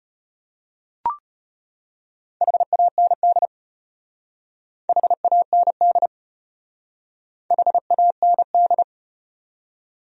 Morse code, heard as a keyed tone, about 33 wpm, 700 Hz. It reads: HAND HAND HAND